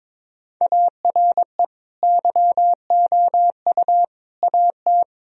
Morse code, heard as a keyed tone, about 22 wpm, 700 Hz. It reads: ARE YOU AT
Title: Sentences from Top 1000 Words - Review Part 2 - 22wpm